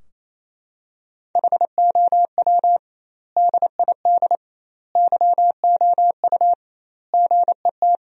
Morse code, heard as a keyed tone, about 28 wpm, 700 Hz. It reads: HOW DID YOU GET
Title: 2x Word-Spacing - Sentences from Top 200 Words - 28wpm